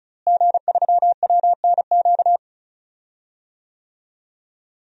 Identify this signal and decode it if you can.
Morse code, heard as a keyed tone, about 36 wpm, 700 Hz. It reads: G3WNQ
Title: DXCC-100 Call Signs - 5 Characters - Repeated 3x - 35wpm